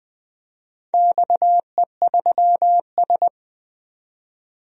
Morse code, heard as a keyed tone, about 20 wpm, 700 Hz. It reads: XE3S
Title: DXCC-100 Call Signs - 4 Characters - 20wpm